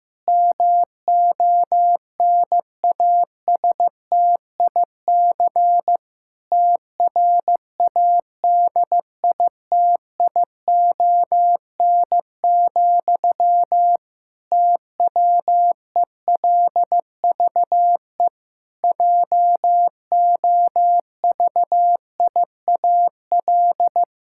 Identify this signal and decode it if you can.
Morse code, heard as a keyed tone, about 15 wpm, 700 Hz. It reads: MONASTIC TRADITION, TWELVE JOVIAL
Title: Pangram Challenge v2 - 15wpm